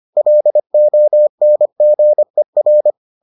Morse code, heard as a keyed tone, about 25 wpm, 600 Hz. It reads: LONGER